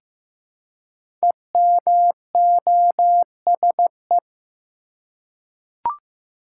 Morse code, heard as a keyed tone, about 15 wpm, 700 Hz. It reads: EMOSE
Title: Code Groups - 5 Letters - 15wpm